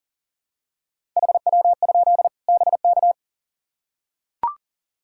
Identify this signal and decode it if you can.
Morse code, heard as a keyed tone, about 39 wpm, 700 Hz. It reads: HW? BK
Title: Practice QSO - Exchange 1 - Beginning - 40wpm